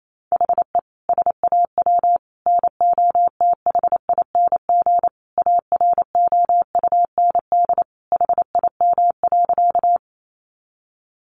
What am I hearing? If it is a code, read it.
Morse code, heard as a keyed tone, about 28 wpm, 700 Hz. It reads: HE SAW NOTHING AROUND HIM.
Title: Sentences from Top 600 Words 28wpm